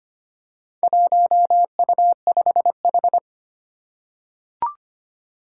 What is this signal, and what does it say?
Morse code, heard as a keyed tone, about 25 wpm, 700 Hz. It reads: 1U5H
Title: Four Challenging Letter-Number Combinations 25wpm